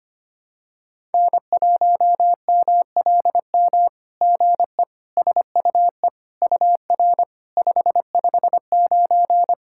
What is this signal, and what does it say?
Morse code, heard as a keyed tone, about 25 wpm, 700 Hz. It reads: N1MLM GE SUE UR 559
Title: SOTA QSO Practice 25wpm